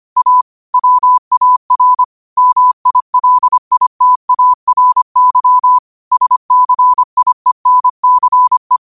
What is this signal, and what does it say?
Morse code, heard as a keyed tone, about 25 wpm, 1000 Hz. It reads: A WAR MILITARY SCIENCE
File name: qotdfiles / 25wpm